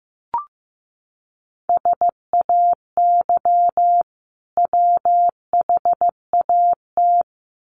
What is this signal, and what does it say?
Morse code, heard as a keyed tone, about 15 wpm, 700 Hz. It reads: SAY WHAT
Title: Sentences from Top 200 Words 15wpm